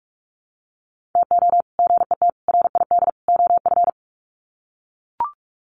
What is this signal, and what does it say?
Morse code, heard as a keyed tone, about 44 wpm, 700 Hz. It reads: TO GET RID OF
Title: Sets of 4 Words 45wpm